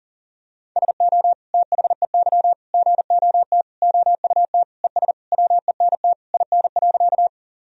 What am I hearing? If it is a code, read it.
Morse code, heard as a keyed tone, about 40 wpm, 700 Hz. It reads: SO THEY GOT OUT ES WENT IN.
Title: Sentences from Top 300 Words and Abbreviations 40wpm